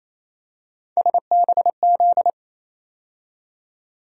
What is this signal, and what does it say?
Morse code, heard as a keyed tone, about 28 wpm, 700 Hz. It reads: SBZ
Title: USA Call Sign Postfixes - 3 Letters 28wpm